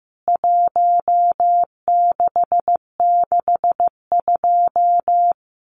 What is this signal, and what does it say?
Morse code, heard as a keyed tone, about 15 wpm, 700 Hz. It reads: 1662
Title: CWT Exchange 15wpm